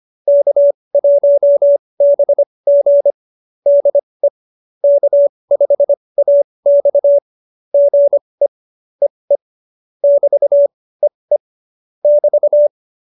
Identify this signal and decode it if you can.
Morse code, heard as a keyed tone, about 25 wpm, spaced as for 20 wpm, 570 Hz. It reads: K1BG DE K5AX GE EE <BT>EE <BT>